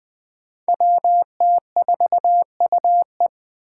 Morse code, heard as a keyed tone, about 20 wpm, 700 Hz. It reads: WT4UE
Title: Five Challenging Letter-Number Combinations 20wpm